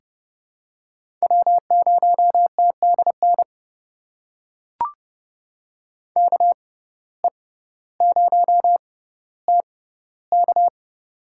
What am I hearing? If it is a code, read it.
Morse code, heard as a keyed tone, about 31 wpm, spaced as for 16 wpm, 700 Hz. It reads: W0TDN KE0TK
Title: USA Call Signs - 5 Characters - 6x Character Spacing - 30wpm